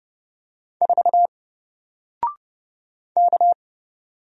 Morse code, heard as a keyed tone, about 29 wpm, 700 Hz. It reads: 4 K
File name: Single-Letter-Number-30wpm